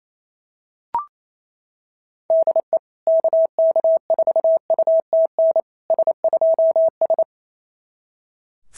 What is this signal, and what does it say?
Morse code, heard as a keyed tone, about 28 wpm, 690 Hz. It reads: DE KK4UTN S2S
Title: SOTA QSO Practice 28wpm